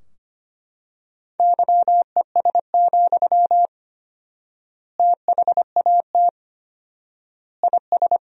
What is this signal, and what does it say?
Morse code, heard as a keyed tone, about 25 wpm, 700 Hz. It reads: YES, THAT IS
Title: 4x Word-Spacing - Sentences from Top 500 Words - 25wpm